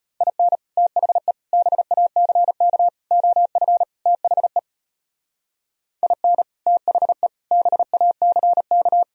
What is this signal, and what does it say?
Morse code, heard as a keyed tone, about 37 wpm, 700 Hz. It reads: IN THE BACK OF THE IN THE BACK